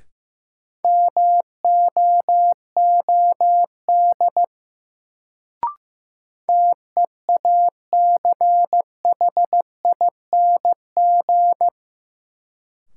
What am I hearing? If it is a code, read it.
Morse code, heard as a keyed tone, about 15 wpm, 700 Hz. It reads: MOOD TEACHING
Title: Sentences from Top 1100 Words - New Word Drill - 15wpm